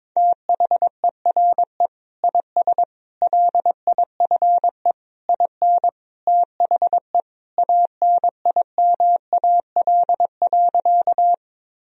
Morse code, heard as a keyed tone, about 22 wpm, 700 Hz. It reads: THERE IS LIFE IN THE ANIMAL.
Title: Sentences from Top 1000 Words - Review Part 1 - 22wpm